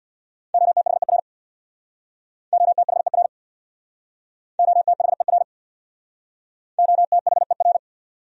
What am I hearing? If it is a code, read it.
Morse code, heard as a keyed tone, about 66 wpm, 700 Hz. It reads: OTHER OTHER OTHER OTHER